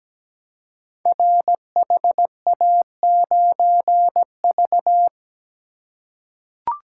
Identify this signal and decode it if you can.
Morse code, heard as a keyed tone, about 17 wpm, 700 Hz. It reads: RHA9V
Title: Code Groups - 5 Letter:Numbers - 17wpm